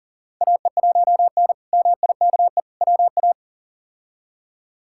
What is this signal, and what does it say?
Morse code, heard as a keyed tone, about 37 wpm, 700 Hz. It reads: AE1N MIKE WA
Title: K1USN SST QSO Practice - 40wpm